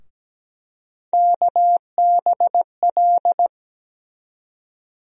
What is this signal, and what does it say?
Morse code, heard as a keyed tone, about 17 wpm, 700 Hz. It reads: KBL
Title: USA Call Sign Postfixes - 3 Letters 17wpm